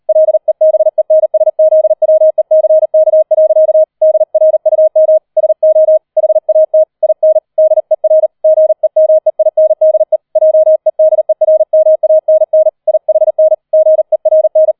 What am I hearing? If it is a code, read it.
Morse code, heard as a keyed tone, about 38 wpm, 630 Hz. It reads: LEBENSZWECK. DRUM SO HAT IN DER GEMEINDE JEDERMANN IHN GERN
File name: 003 Dritter Streich 40wpm